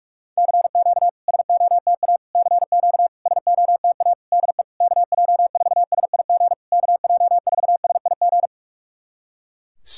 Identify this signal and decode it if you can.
Morse code, heard as a keyed tone, about 44 wpm, 700 Hz. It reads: CQ SOTA CQ SOTA DE KJ4SIG KJ4SIG
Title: SOTA QSO Practice 45wpm